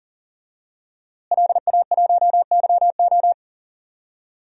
Morse code, heard as a keyed tone, about 37 wpm, 700 Hz. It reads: LA1YO
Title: DXCC-100 Call Signs - 5 Characters - Repeated 3x - 40wpm